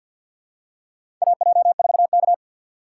Morse code, heard as a keyed tone, about 52 wpm, 700 Hz. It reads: AJ4K
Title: USA Call Signs - 4 Characters - 3x Character Spacing - 50wpm